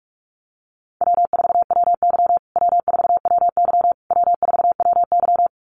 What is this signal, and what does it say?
Morse code, heard as a keyed tone, about 44 wpm, 700 Hz. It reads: W4WY W4WY W4WY
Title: USA Call Signs Repeated 3x 45wpm